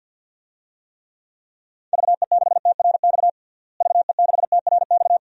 Morse code, heard as a keyed tone, about 52 wpm, 700 Hz. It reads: VE6TRX VE6TRX
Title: DXCC-100 Call Signs - 6 Characters - Repeated 2x - 50wpm